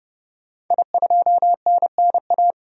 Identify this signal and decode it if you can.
Morse code, heard as a keyed tone, about 30 wpm, 700 Hz. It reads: I2NNA
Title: DXCC-100 Call Signs - 5 Characters - 30wpm